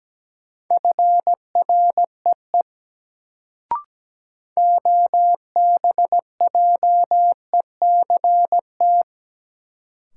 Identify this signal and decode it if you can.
Morse code, heard as a keyed tone, about 17 wpm, 700 Hz. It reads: FREE OBJECT
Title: Sentences from Top 500 Words - New Word Drill - 17wpm